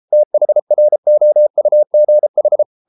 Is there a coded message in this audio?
Yes: Morse code from 0.1 to 2.6 s, heard as a keyed tone, about 33 wpm, 600 Hz. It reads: THROUGH